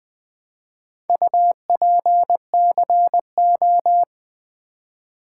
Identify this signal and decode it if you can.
Morse code, heard as a keyed tone, about 20 wpm, 700 Hz. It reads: UPCO
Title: Build A Word - 2 Letters At A Time - 8-Letter Words - 20wpm